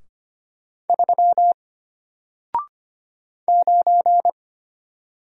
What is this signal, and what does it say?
Morse code, heard as a keyed tone, about 25 wpm, 700 Hz. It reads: 3 9
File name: Single-Letter-Number-25wpm